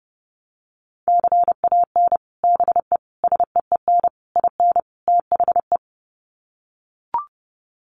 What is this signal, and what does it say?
Morse code, heard as a keyed tone, about 30 wpm, 700 Hz. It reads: CAN BE SEEN IN THE
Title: Sets of 5 Words Encore2 30wpm